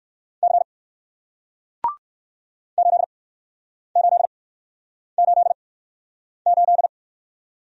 Morse code, heard as a keyed tone, about 60 wpm, spaced as for 9 wpm, 700 Hz. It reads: C 8888